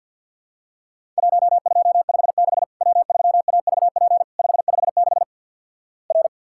Morse code, heard as a keyed tone, about 50 wpm, 700 Hz. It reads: 1256 W3AVP 556 R
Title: Worked All Europe DX Contest - QTC Exchange with Repeats - 50wpm